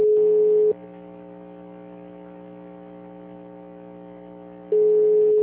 Ringback tone (Europe, 425 Hz).